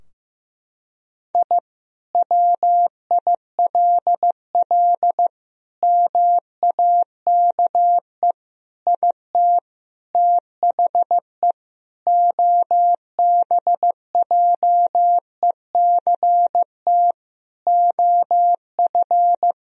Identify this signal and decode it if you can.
Morse code, heard as a keyed tone, about 15 wpm, 700 Hz. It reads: I WILL MAKE IT THE OBJECT OF
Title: Sentences from Top 1000 Words - Review Part 3 - 15wpm